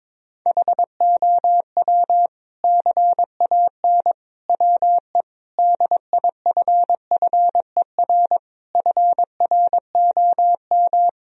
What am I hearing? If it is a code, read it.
Morse code, heard as a keyed tone, about 22 wpm, 700 Hz. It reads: HOW CAN WE DIFFER FROM
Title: Sentences from Top 1000 Words - Review Part 1 - 22wpm